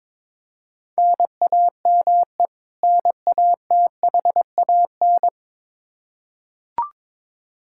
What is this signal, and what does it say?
Morse code, heard as a keyed tone, about 22 wpm, 700 Hz. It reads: NAME NATHAN
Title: Practice QSO - Exchange 1 - Beginning - 22wpm